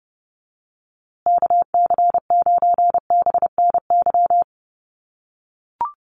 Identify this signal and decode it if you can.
Morse code, heard as a keyed tone, about 30 wpm, 700 Hz. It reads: KC9BNY
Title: USA Call Signs Repeated 2x 30wpm